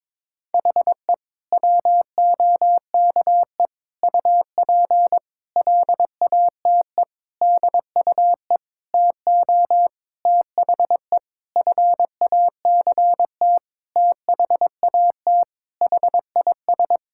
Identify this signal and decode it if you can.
Morse code, heard as a keyed tone, about 22 wpm, 700 Hz. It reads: HE WOKE UP LATE DUE TO THE FACT THAT HIS
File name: practice / 22wpm / Sentences with 4-Word Phrases - 22wpm